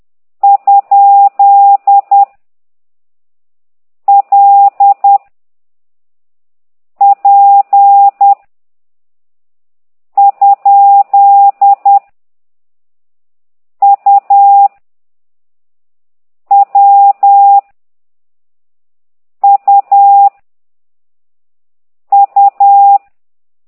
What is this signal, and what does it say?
Morse code, heard as a keyed tone, about 10 wpm, 800 Hz. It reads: ? L P ? U W U U
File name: ct2idw / exerc13